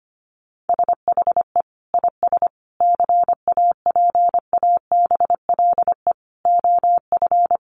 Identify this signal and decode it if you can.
Morse code, heard as a keyed tone, about 25 wpm, 700 Hz. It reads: SHE IS CAPABLE OF
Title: Sentences from Top 1200 Words 25wpm